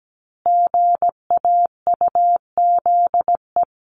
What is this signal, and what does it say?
Morse code, heard as a keyed tone, about 17 wpm, 700 Hz. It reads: GAUZE
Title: A Sound Of Thunder 17wpm